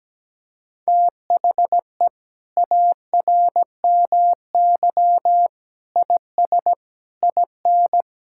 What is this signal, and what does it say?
Morse code, heard as a keyed tone, about 17 wpm, 700 Hz. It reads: THE ARMY IS IN
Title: Sentences from Top 1000 Words - Review Part 6 - 17wpm